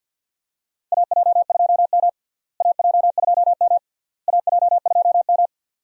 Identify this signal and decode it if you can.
Morse code, heard as a keyed tone, about 51 wpm, 700 Hz. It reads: AJ2M AJ2M AJ2M
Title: USA Call Signs - 4 Characters - Repeated 3x - 50wpm